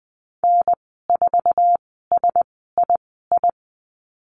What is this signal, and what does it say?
Morse code, heard as a keyed tone, about 20 wpm, spaced as for 14 wpm, 700 Hz. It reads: N4SII
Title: 069 Lesson 12g - Letter d - Call Signs 20wpm